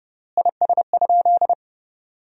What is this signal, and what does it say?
Morse code, heard as a keyed tone, about 31 wpm, 700 Hz. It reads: IS?
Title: 3x Word-Spacing - Sentences from Top 100 Words - 30wpm